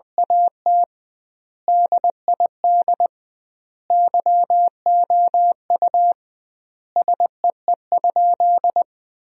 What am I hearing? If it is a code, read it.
Morse code, heard as a keyed tone, about 21 wpm, 700 Hz. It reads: AT DID YOU SEE?
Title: 2x Word-Spacing - Sentences from Top 100 Words - 20wpm